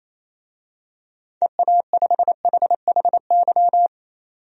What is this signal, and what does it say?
Morse code, heard as a keyed tone, about 28 wpm, 700 Hz. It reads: EA5HHY
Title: DXCC-100 Call Signs - 6 Characters - 28wpm